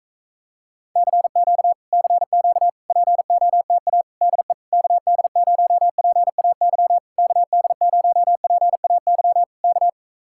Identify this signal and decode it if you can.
Morse code, heard as a keyed tone, about 45 wpm, 700 Hz. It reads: CQ CQ POTA DE KD0PAY KD0PAY K